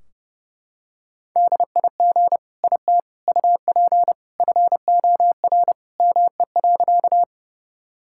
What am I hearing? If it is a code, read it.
Morse code, heard as a keyed tone, about 30 wpm, 700 Hz. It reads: DIG IT UP FOR ME.
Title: Sentences from Top 1600 Words 30wpm